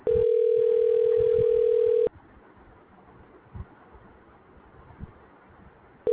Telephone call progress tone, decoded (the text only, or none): ringback tone